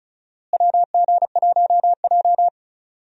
Morse code, heard as a keyed tone, about 36 wpm, 700 Hz. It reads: WG1J
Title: USA Call Signs - 4 Characters - Repeated 3x - 35wpm